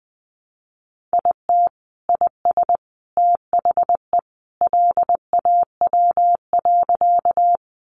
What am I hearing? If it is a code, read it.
Morse code, heard as a keyed tone, about 20 wpm, 700 Hz. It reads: IT IS THE LAW.
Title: Sentences from Top 600 Words 20wpm